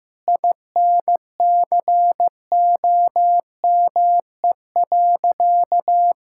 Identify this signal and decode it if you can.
Morse code, heard as a keyed tone, about 15 wpm, 700 Hz. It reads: INCOME.
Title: Sentences from Top 1000 Words - Review Part 6 - 15wpm